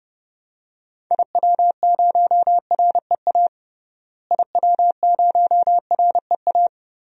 Morse code, heard as a keyed tone, about 30 wpm, 700 Hz. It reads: IW0REA IW0REA
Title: DXCC-100 Call Signs - 6 Characters - Repeated 2x - 30wpm